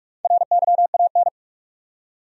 Morse code, heard as a keyed tone, about 46 wpm, 700 Hz. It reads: RYAN
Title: USA Common Names - 45wpm